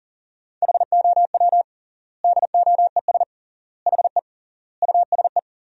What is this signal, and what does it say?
Morse code, heard as a keyed tone, about 37 wpm, 700 Hz. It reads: HOW DOES HE USE
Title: 3x Word-Spacing - Sentences from Top 200 Words - 40wpm